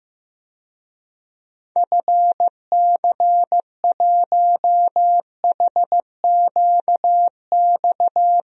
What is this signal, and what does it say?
Morse code, heard as a keyed tone, about 16 wpm, 700 Hz. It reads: FC1HQX
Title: DXCC-100 Call Signs - 6 Characters - 15wpm